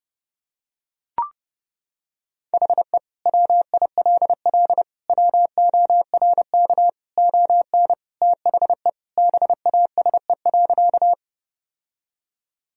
Morse code, heard as a keyed tone, about 30 wpm, 700 Hz. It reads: HE WILL WORK ON THE BASE.
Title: Sentences from Top 1000 Words - Review Part 1 - 30wpm